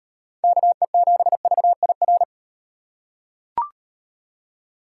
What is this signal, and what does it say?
Morse code, heard as a keyed tone, about 36 wpm, 700 Hz. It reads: KE7VIR